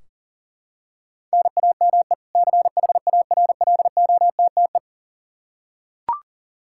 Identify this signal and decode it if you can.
Morse code, heard as a keyed tone, about 37 wpm, 700 Hz. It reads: NAME CHARLOTTE
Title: Practice QSO - Exchange 1 - Beginning - 40wpm